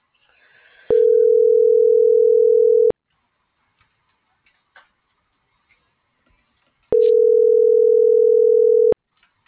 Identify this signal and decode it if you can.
Ringback tone (North America), 440 + 480 Hz.